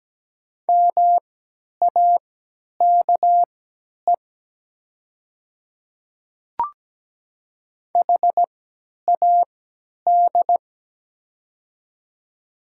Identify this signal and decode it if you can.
Morse code, heard as a keyed tone, about 17 wpm, spaced as for 10 wpm, 700 Hz. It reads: MAKE HAD
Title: 3x Character Spacing - Top 300 Words - 17wpm